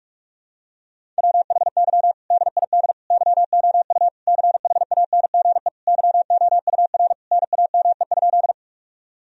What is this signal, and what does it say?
Morse code, heard as a keyed tone, about 41 wpm, 700 Hz. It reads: WHY DID YOU CHANGE YOUR NAME?
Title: Sentences from Top 1000 Words - Review Part 1 - 45wpm